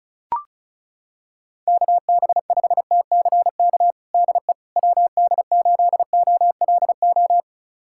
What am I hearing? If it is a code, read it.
Morse code, heard as a keyed tone, about 36 wpm, 690 Hz. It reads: KB5TCK DE WD8OLO
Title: USA Callsign DE Callsign - 6 Characters - 35wpm